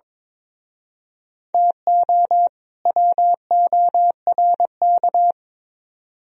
Morse code, heard as a keyed tone, about 26 wpm, 700 Hz. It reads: TO WORK
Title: Speed-Racing - Sets of 2 Words 22wpm